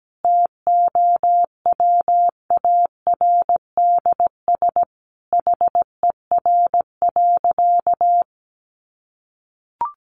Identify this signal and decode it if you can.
Morse code, heard as a keyed tone, about 17 wpm, 700 Hz. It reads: TOWARDS HER.
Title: Sentences from Top 1200 Words 17wpm